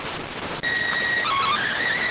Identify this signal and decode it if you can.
SSTV (slow-scan television), PD 290.